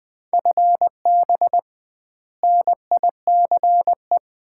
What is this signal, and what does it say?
Morse code, heard as a keyed tone, about 20 wpm, 700 Hz. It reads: FB NICE
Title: Common QSO Phrases - 2x Word-Spacing - 20wpm